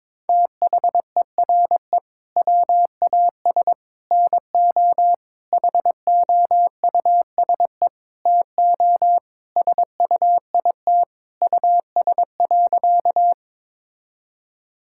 Morse code, heard as a keyed tone, about 22 wpm, 700 Hz. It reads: THERE WAS NO HOUSE TO SUIT US.
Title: Sentences from Top 1600 Words 22wpm